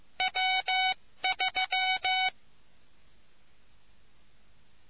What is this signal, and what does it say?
Morse code, heard as a keyed tone, about 15 wpm, 750 Hz. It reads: W3